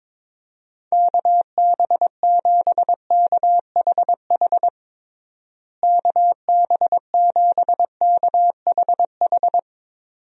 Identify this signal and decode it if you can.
Morse code, heard as a keyed tone, about 21 wpm, 700 Hz. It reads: KB7KHH KB7KHH
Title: USA Call Signs - 6 Characters - Repeated 2x - 22wpm